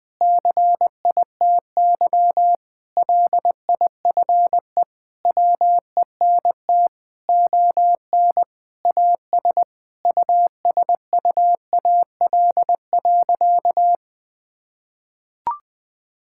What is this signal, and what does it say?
Morse code, heard as a keyed tone, about 21 wpm, 700 Hz. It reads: CITY LIFE WENT ON AS USUAL.
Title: Sentences with 5-Word Phrases - 20wpm